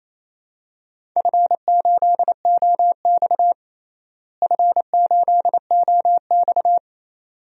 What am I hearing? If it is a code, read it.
Morse code, heard as a keyed tone, about 27 wpm, 700 Hz. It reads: F8OX F8OX
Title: DXCC-100 Call Signs - 4 Characters - Repeated 2x - 28wpm